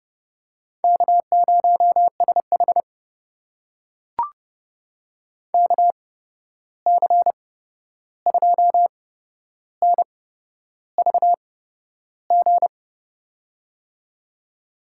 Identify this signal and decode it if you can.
Morse code, heard as a keyed tone, about 32 wpm, spaced as for 11 wpm, 700 Hz. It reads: K0SH KC2NVG
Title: USA Call Signs - 8x Character Spacing - 30wpm